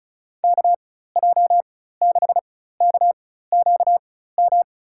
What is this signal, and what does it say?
Morse code, heard as a keyed tone, about 36 wpm, spaced as for 16 wpm, 700 Hz. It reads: KJ6KQM
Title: USA Call Signs - 4x Character Spacing - 35wpm